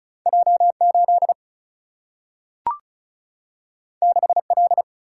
Morse code, heard as a keyed tone, about 36 wpm, 700 Hz. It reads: J8 6L
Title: Single-Letter Single-Number 35wpm